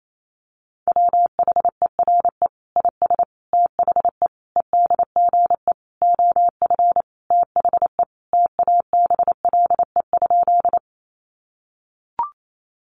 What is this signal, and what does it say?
Morse code, heard as a keyed tone, about 28 wpm, 700 Hz. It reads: WHERE IS THE EDGE OF THE TABLE?
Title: Sentences from Top 1300 Words 28wpm